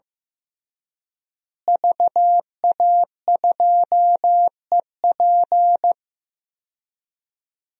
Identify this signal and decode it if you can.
Morse code, heard as a keyed tone, about 15 wpm, 700 Hz. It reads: VA2EP
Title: DXCC-100 Call Signs - 5 Characters - Repeated 2x - 15wpm